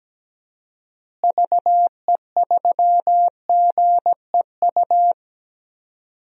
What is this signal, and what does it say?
Morse code, heard as a keyed tone, about 17 wpm, 700 Hz. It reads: VE3GEU
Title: DXCC-100 Call Signs - 6 Characters - Repeated 3x - 17wpm